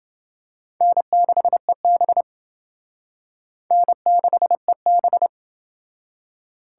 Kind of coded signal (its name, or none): Morse code